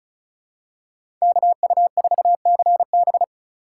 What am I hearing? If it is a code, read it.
Morse code, heard as a keyed tone, about 34 wpm, 700 Hz. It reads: KU4CB
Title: USA Call Signs - 3x Character Spacing - 35wpm